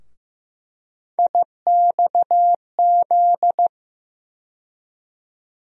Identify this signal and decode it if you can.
Morse code, heard as a keyed tone, about 15 wpm, 700 Hz. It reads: IXZ
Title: USA Call Sign Postfixes - 3 Letters 15wpm